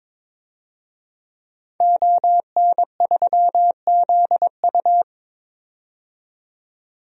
Morse code, heard as a keyed tone, about 22 wpm, 700 Hz. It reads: ON3ZU
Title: DXCC-100 Call Signs - 5 Characters - 22wpm